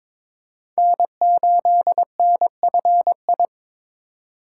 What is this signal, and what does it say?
Morse code, heard as a keyed tone, about 22 wpm, 700 Hz. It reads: N8NFI
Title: USA Call Signs - 3x Character Spacing - 22wpm